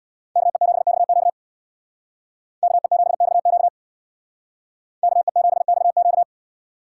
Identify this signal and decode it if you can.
Morse code, heard as a keyed tone, about 59 wpm, 700 Hz. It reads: KE7XQ KE7XQ KE7XQ
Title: Speed Racing - USA 2x2 Call Signs 50wpm